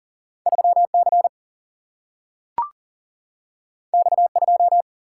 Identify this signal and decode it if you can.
Morse code, heard as a keyed tone, about 40 wpm, 700 Hz. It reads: 3C X2
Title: Single-Letter Single-Number 40wpm